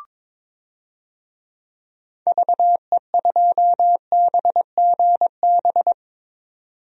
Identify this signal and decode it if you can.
Morse code, heard as a keyed tone, about 21 wpm, 700 Hz. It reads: VE2BGB
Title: DXCC-100 Call Signs - 6 Characters - 22wpm